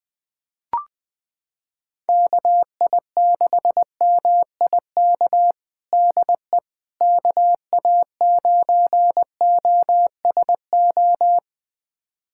Morse code, heard as a keyed tone, about 21 wpm, 700 Hz. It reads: KI6MIK DE KA9OSO
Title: USA Callsign DE Callsign - 6 Characters - 20wpm